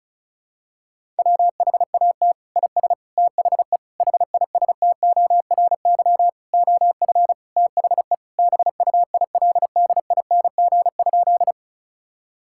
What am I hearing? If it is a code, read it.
Morse code, heard as a keyed tone, about 36 wpm, 700 Hz. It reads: WHAT IS THE HISTORY OF THE BUILDING?
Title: Sentences from Top 1000 Words - Review Part 5 - 35wpm